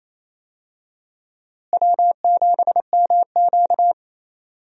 Morse code, heard as a keyed tone, about 28 wpm, 700 Hz. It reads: W7MQ
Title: USA Call Signs - 4 Characters - Repeated 3x - 28wpm